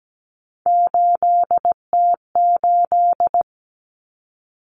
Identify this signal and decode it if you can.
Morse code, heard as a keyed tone, about 17 wpm, 700 Hz. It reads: 8T8
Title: Common Cut Numbers - 3-Digit 17wpm